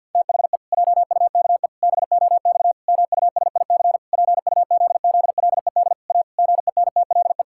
Morse code, heard as a keyed tone, about 50 wpm, 700 Hz. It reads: THE JUKE BOX MUSIC PUZZLED A GENTLE
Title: Pangram Challenge v2 - 50wpm